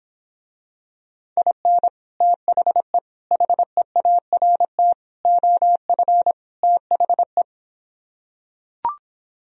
Morse code, heard as a keyed tone, about 26 wpm, 700 Hz. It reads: IN THE HEART OF THE